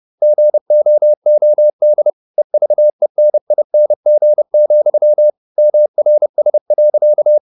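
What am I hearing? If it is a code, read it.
Morse code, heard as a keyed tone, about 30 wpm, 600 Hz. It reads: GOOD EVENING, MRS.
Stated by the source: Morse code at 30 wpm, the novel Never by Ken Follett in English, from CW med Gnister.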